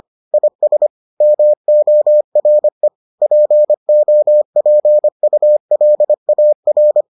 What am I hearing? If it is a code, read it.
Morse code, heard as a keyed tone, about 25 wpm, 600 Hz. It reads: IS MORE POPULAR